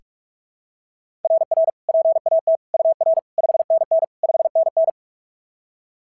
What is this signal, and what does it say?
Morse code, heard as a keyed tone, about 44 wpm, 690 Hz. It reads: RR PAT UR 5NN 5NN
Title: SOTA QSO Practice 45wpm